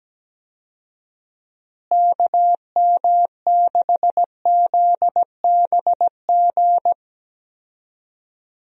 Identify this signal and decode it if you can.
Morse code, heard as a keyed tone, about 17 wpm, 700 Hz. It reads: KM6ZBG
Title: USA Call Signs - 6 Characters - Repeated 2x - 17wpm